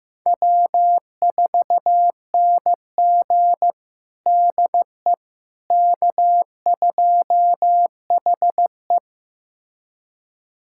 Morse code, heard as a keyed tone, about 16 wpm, 700 Hz. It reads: W4NG DE K2HE
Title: USA Callsign DE Callsign - 4 Characters - 15wpm